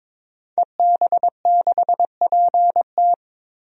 Morse code, heard as a keyed tone, about 22 wpm, 700 Hz. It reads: EB6PT
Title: DXCC-100 Call Signs - 5 Characters - 22wpm